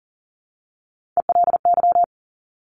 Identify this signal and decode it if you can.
Morse code, heard as a keyed tone, about 38 wpm, 690 Hz. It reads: ELY
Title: Top Three Letter Combinations in Words 40wpm